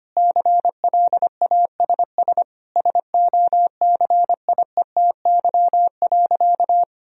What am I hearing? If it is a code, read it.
Morse code, heard as a keyed tone, about 25 wpm, 700 Hz. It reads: CLASS SOCIETY.
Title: Sentences from Top 1000 Words - Review Part 4 - 25wpm